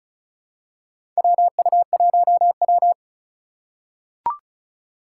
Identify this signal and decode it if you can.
Morse code, heard as a keyed tone, about 34 wpm, 700 Hz. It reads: WU1W
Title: USA Call Signs - 4 Characters - 2x Character Spacing - 35wpm